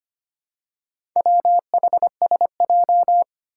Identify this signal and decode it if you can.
Morse code, heard as a keyed tone, about 25 wpm, 700 Hz. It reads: WHSJ
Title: Four Challenging Letter-Number Combinations 25wpm